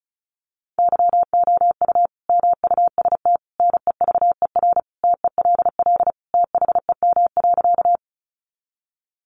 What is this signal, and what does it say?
Morse code, heard as a keyed tone, about 34 wpm, 700 Hz. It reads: YOU MUST NEVER TELL THEM.
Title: Sentences from Top 900 Words 35wpm